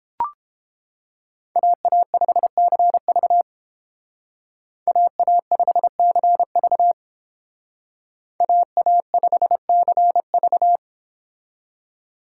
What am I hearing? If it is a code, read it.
Morse code, heard as a keyed tone, about 29 wpm, 700 Hz. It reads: AA5CV AA5CV AA5CV